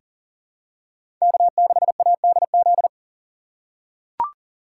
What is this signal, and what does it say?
Morse code, heard as a keyed tone, about 40 wpm, 700 Hz. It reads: K6ADZ
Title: USA Call Signs - 5 Characters - 2x Character Spacing - 40wpm